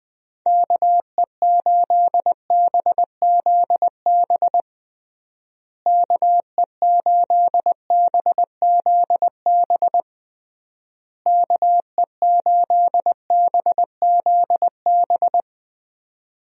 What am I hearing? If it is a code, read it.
Morse code, heard as a keyed tone, about 20 wpm, 700 Hz. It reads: KE8BZB KE8BZB KE8BZB
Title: USA Call Signs - 6 Characters - Repeated 3x - 20wpm